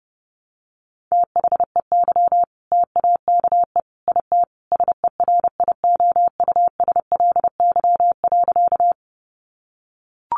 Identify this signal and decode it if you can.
Morse code, heard as a keyed tone, about 30 wpm, 700 Hz. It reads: THEY TAKE IT SERIOUSLY.
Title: Sentences from Top 1200 Words 30wpm